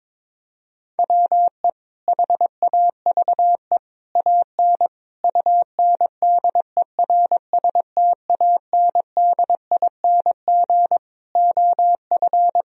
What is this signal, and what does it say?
Morse code, heard as a keyed tone, about 22 wpm, 700 Hz. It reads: WE HAVE AN UNDERSTANDING OF
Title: Sentences from Top 1000 Words - Review Part 3 - 22wpm